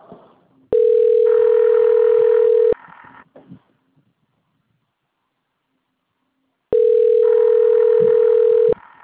Ringback tone (North America), 440 + 480 Hz.